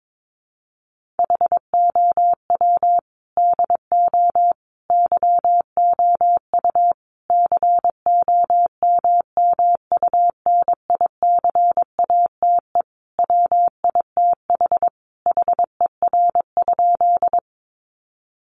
Morse code, heard as a keyed tone, about 22 wpm, 700 Hz. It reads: HOW DO YOU COMMUNICATE WITH HER?
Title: Sentences from Top 1300 Words 22wpm